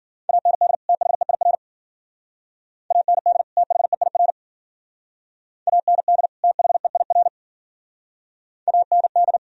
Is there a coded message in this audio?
Yes: Morse code from 0.3 to 9.4 s, heard as a keyed tone, about 52 wpm, 700 Hz. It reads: AND THEIR AND THEIR AND THEIR AND